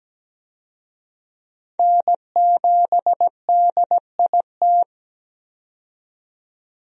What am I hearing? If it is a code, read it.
Morse code, heard as a keyed tone, about 17 wpm, 700 Hz. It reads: N7DIT
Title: USA Call Signs - 5 Characters - 17wpm